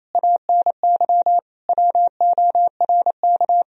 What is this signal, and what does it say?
Morse code, heard as a keyed tone, about 28 wpm, 700 Hz. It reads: ANY WORK
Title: Sentences from Top 1000 Words - Review Part 3 - 28wpm